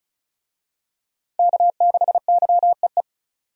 Morse code, heard as a keyed tone, about 36 wpm, 700 Hz. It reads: K6YEE
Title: USA Call Signs - 5 Characters - 35wpm